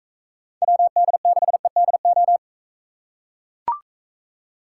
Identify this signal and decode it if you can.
Morse code, heard as a keyed tone, about 43 wpm, 700 Hz. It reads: WD6EDO